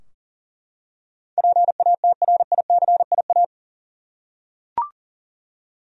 Morse code, heard as a keyed tone, about 40 wpm, 700 Hz. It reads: PATRICIA
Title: USA Common Names - 40wpm